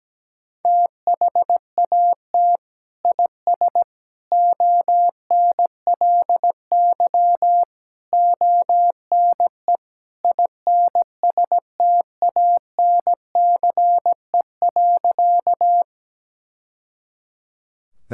Morse code, heard as a keyed tone, about 17 wpm, 700 Hz. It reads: THAT IS ONLY ONE INSTANCE.
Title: Sentences from Top 1000 Words - Review Part 5 - 17wpm